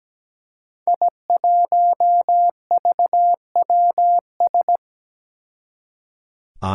Morse code, heard as a keyed tone, about 17 wpm, 700 Hz. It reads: I1VWS
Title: Five Challenging Letter-Number Combinations 17wpm